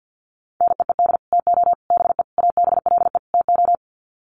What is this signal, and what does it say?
Morse code, heard as a keyed tone, about 50 wpm, 700 Hz. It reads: NEED TO BE ABLE TO
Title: Sets of 5 Words Encore 50wpm